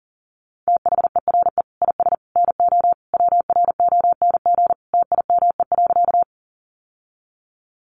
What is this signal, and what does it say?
Morse code, heard as a keyed tone, about 40 wpm, 700 Hz. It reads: THERE IS NO WRONG TIME.
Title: Sentences from Top 900 Words 40wpm